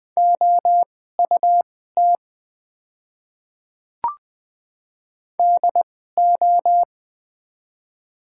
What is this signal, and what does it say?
Morse code, heard as a keyed tone, about 20 wpm, spaced as for 14 wpm, 700 Hz. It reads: OUT DO
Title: 2x Character Spacing - Top 100 Words - 20wpm